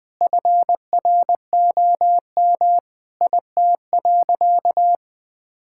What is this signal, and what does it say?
Morse code, heard as a keyed tone, about 20 wpm, 700 Hz. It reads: FROM IT.
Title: Sentences from Top 1000 Words - Review Part 5 - 20wpm